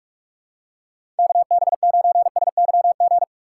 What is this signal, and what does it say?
Morse code, heard as a keyed tone, about 44 wpm, 700 Hz. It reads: KB9SYG
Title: USA Call Signs - 6 Characters - 6x Character Spacing - 45wpm